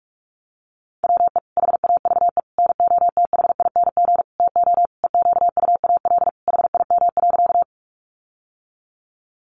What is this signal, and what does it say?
Morse code, heard as a keyed tone, about 44 wpm, 700 Hz. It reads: WE HAVE NOTHING TO EQUAL HIM.
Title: Sentences from Top 1400 Words 45wpm